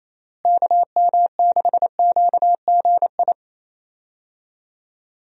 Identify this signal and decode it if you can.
Morse code, heard as a keyed tone, about 27 wpm, 700 Hz. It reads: KM6QGI
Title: USA Call Signs - 6 Characters - Repeated 2x - 28wpm